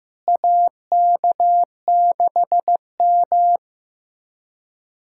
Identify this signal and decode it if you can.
Morse code, heard as a keyed tone, about 15 wpm, 700 Hz. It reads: AK6M
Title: USA Callsign DE Callsign - 4 Characters - 15wpm